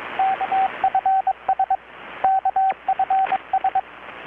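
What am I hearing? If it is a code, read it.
Morse code, heard as a keyed tone, about 22 wpm, 770 Hz, with radio noise. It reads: KFS KFS
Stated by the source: Morse from KPH's Night of Nights XX maritime radiotelegraphy event, received in Chiapas, Mexico about 3,800 km away on a Kenwood R-600 with a grounded 10 m long-wire antenna.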